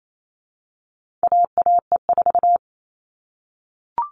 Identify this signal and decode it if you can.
Morse code, heard as a keyed tone, about 28 wpm, 700 Hz. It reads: AAE4
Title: All Cut Numbers - 4-Digit 28wpm